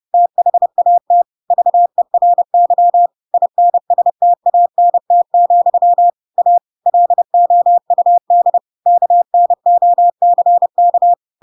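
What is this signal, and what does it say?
Morse code, heard as a keyed tone, about 30 wpm, 700 Hz. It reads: THAT VERY INSTANT, A LOUD KNOCK
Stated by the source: Chapter 2 of The Adventures of Pinocchio read as a book in Morse code (CW) at 30 wpm, by ik1hge.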